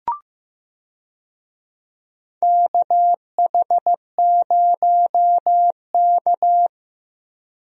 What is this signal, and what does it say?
Morse code, heard as a keyed tone, about 15 wpm, 700 Hz. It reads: KH0K
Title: USA Call Signs - 4 Characters - Repeated 2x - 15wpm